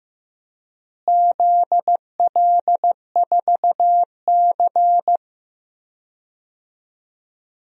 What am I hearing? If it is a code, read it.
Morse code, heard as a keyed tone, about 15 wpm, 700 Hz. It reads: ZL4C
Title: DXCC-100 Call Signs - 4 Characters - Repeated 2x - 15wpm